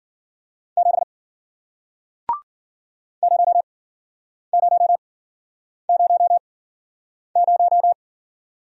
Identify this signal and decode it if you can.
Morse code, heard as a keyed tone, about 55 wpm, spaced as for 9 wpm, 700 Hz. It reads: 7 0000